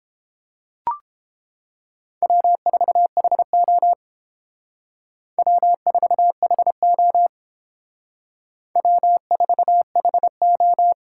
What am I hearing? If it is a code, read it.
Morse code, heard as a keyed tone, about 28 wpm, 700 Hz. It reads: W4HO W4HO W4HO